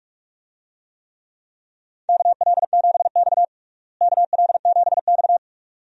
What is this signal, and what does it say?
Morse code, heard as a keyed tone, about 46 wpm, 700 Hz. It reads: KL7X KL7X
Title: DXCC-100 Call Signs - 4 Characters - Repeated 2x - 45wpm